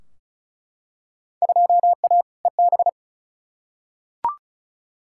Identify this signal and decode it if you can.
Morse code, heard as a keyed tone, about 36 wpm, 700 Hz. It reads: 2A EB
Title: Field Day Exchanges - 35wpm